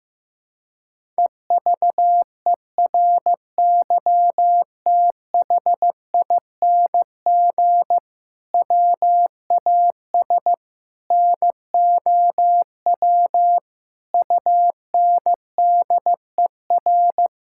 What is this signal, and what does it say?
Morse code, heard as a keyed tone, about 16 wpm, 700 Hz. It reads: EVERYTHING WAS NOW UNDER
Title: Sentences from Top 1000 Words - Review Part 5 - 15wpm